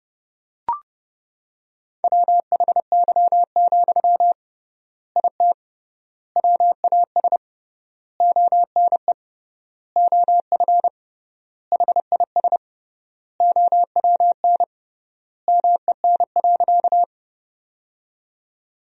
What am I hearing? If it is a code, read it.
Morse code, heard as a keyed tone, about 29 wpm, 700 Hz. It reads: WHY, IT WAS ONE OF HIS OWN MEN.
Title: 3x Word-Spacing - Sentences from Top 300 Words - 30wpm